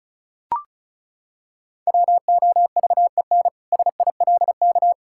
Morse code, heard as a keyed tone, about 36 wpm, 700 Hz. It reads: WOVEN SILK
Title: Pangram Challenge v2 - 35wpm